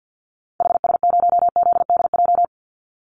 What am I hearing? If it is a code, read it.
Morse code, heard as a keyed tone, about 50 wpm, 700 Hz. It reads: HS0ZDJ
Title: Call Signs - 50wpm